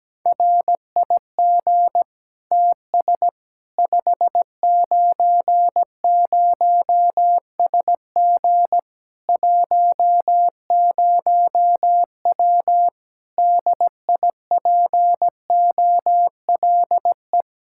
Morse code, heard as a keyed tone, about 17 wpm, 700 Hz. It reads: RIG TS 590SG 10W DIPOLE
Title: K1USN SST QSO Practice 17wpm